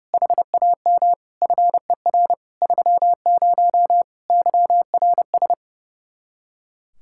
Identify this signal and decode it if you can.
Morse code, heard as a keyed tone, about 37 wpm, 700 Hz. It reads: HAM FER 30 YRS
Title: Speed-Racing Common QSO Phrases - 30wpm